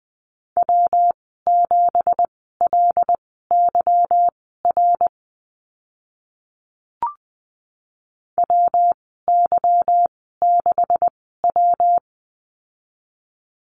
Morse code, heard as a keyed tone, about 20 wpm, spaced as for 15 wpm, 700 Hz. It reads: W7LYR WY6W
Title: 167 Lesson 27g - Number 7 - Call Signs 20wpm